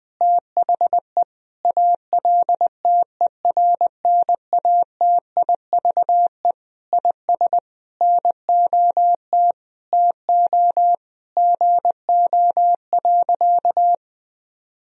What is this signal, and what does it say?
Morse code, heard as a keyed tone, about 21 wpm, 700 Hz. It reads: THE ALTERNATIVE IS NOT TO GO.
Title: Sentences from Top 1000 Words - Review Part 4 - 20wpm